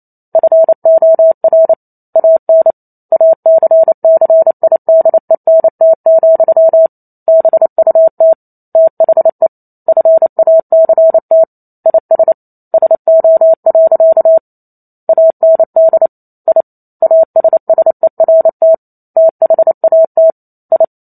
Morse code, heard as a keyed tone, about 28 wpm, 650 Hz. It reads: FOR AN ACCIDENT, BUT THE FACT IS SO. AND I ASSERT THAT I
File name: Chapter 2 - 28 WPM